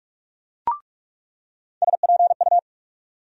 Morse code, heard as a keyed tone, about 42 wpm, 700 Hz. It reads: SPU
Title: USA Call Sign Postfixes - 3 Letters 45wpm